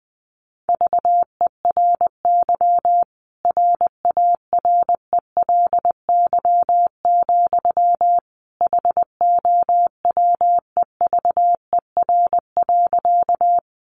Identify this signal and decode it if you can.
Morse code, heard as a keyed tone, about 20 wpm, 700 Hz. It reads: VERY RARELY, HOWEVER.
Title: Sentences from Top 1400 Words 20wpm